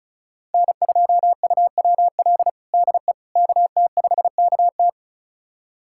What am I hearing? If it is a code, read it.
Morse code, heard as a keyed tone, about 36 wpm, 700 Hz. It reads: N2UWL DE KT5KT
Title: USA Callsign DE Callsign - 5 Characters - 35wpm